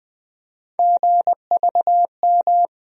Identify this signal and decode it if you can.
Morse code, heard as a keyed tone, about 20 wpm, 700 Hz. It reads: GVM
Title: USA Call Sign Postfixes - 3 Letters 20wpm